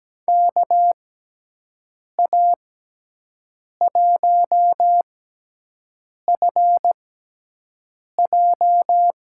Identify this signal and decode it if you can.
Morse code, heard as a keyed tone, about 17 wpm, spaced as for 6 wpm, 700 Hz. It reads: KA1FJ
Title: USA Call Signs - 6 Characters - 6x Character Spacing - 17wpm